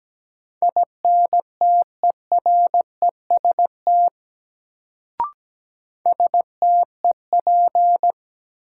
Morse code, heard as a keyed tone, about 17 wpm, 700 Hz. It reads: INTEREST STEP
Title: Sentences from Top 400 Words - New Word Drill - 17wpm